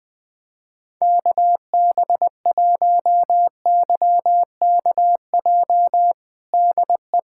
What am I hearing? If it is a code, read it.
Morse code, heard as a keyed tone, about 21 wpm, 700 Hz. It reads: KB1YKJ DE
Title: USA Callsign DE Callsign - 6 Characters - 20wpm